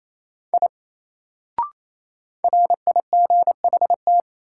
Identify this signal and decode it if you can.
Morse code, heard as a keyed tone, about 28 wpm, 700 Hz. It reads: I RIGHT
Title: Sentences from Top 200 Words - New Word Drill - 28wpm